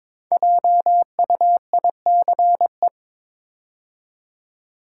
Morse code, heard as a keyed tone, about 22 wpm, 700 Hz. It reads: JUICE
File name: 5 Letter Words - First 3 Letters Given - 22wpm